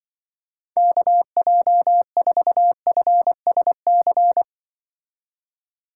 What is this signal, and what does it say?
Morse code, heard as a keyed tone, about 24 wpm, 700 Hz. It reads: KJ4FSC
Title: Speed Racing - USA 2x3 Call Signs 20wpm